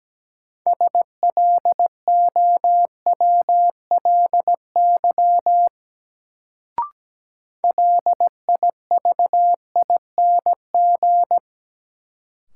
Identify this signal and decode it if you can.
Morse code, heard as a keyed tone, about 17 wpm, 700 Hz. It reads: SLOWLY LIVING
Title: Sentences from Top 1100 Words - New Word Drill Alt - 17wpm